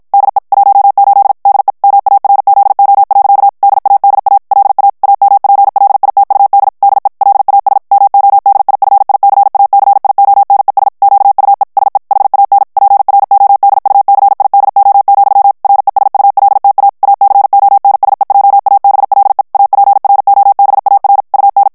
Morse code, heard as a keyed tone, about 56 wpm, 800 Hz. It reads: DE 18 DE MARZO. DADA LA AMPLITUD DE LAS MODIFICACIONES QUE SE HAN PRODUCIDO, RESULTA ACONSEJABLE APROBAR UN
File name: morse mp3 / ESPAGNOL TEXTE EN CLAIR 2 VITESSE 50 MOTS MINUTES